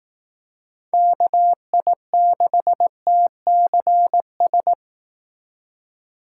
Morse code, heard as a keyed tone, about 18 wpm, 700 Hz. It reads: KI6TCS
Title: Speed Racing - USA 2x3 Call Signs 15wpm